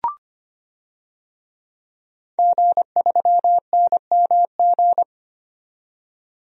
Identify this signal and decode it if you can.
Morse code, heard as a keyed tone, about 25 wpm, 700 Hz. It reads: G3NMG
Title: DXCC-100 Call Signs - 5 Characters - 25wpm